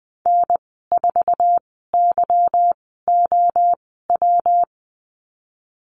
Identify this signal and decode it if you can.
Morse code, heard as a keyed tone, about 20 wpm, spaced as for 15 wpm, 700 Hz. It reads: N4YOW
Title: 147 Lesson 24g - Letter y - Call Signs 20wpm